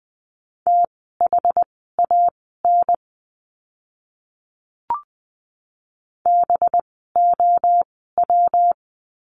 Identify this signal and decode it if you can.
Morse code, heard as a keyed tone, about 20 wpm, spaced as for 15 wpm, 700 Hz. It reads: THAN BOW
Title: 185 Lesson 30f - Letter b - Words 20wpm